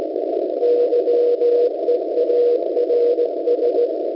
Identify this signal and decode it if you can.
Morse code, heard as a keyed tone, about 17 wpm, 480 Hz, with noise. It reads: YEARS